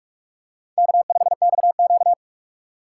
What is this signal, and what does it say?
Morse code, heard as a keyed tone, about 46 wpm, 700 Hz. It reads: K5XQ
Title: K1USN SST QSO Practice - 45wpm